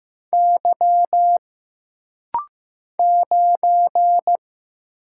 Morse code, heard as a keyed tone, about 15 wpm, 700 Hz. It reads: Y 9
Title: Single Letter-Number Alt - Rapid-Fire 15wpm